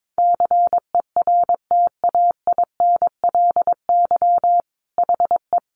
Morse code, heard as a keyed tone, about 22 wpm, 700 Hz. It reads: CERTAINLY HE
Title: Sentences from Top 900 Words 22wpm